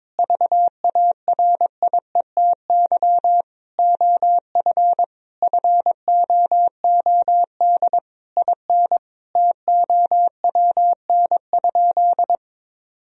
Morse code, heard as a keyed tone, about 22 wpm, 700 Hz. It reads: VARIETY OF FOOD IN TOWN?
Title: Sentences from Top 1000 Words - Review Part 4 - 22wpm